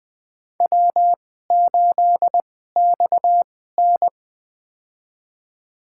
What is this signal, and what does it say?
Morse code, heard as a keyed tone, about 20 wpm, spaced as for 15 wpm, 700 Hz. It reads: W8XN
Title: USA Call Signs - 4 Characters - 2x Character Spacing - 20wpm